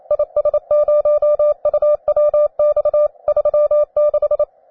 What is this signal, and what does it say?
Morse code, heard as a keyed tone, about 28 wpm, 600 Hz. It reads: IS0UWX 36
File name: wae / cw beginner / jh3aiu cw